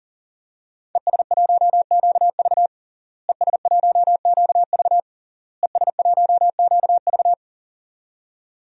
Morse code, heard as a keyed tone, about 40 wpm, 700 Hz. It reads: ES1QV ES1QV ES1QV
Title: DXCC-100 Call Signs - 5 Characters - Repeated 3x - 40wpm